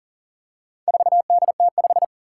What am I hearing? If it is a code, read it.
Morse code, heard as a keyed tone, about 41 wpm, 700 Hz. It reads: 4DT5
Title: Four Challenging Letter-Number Combinations 40wpm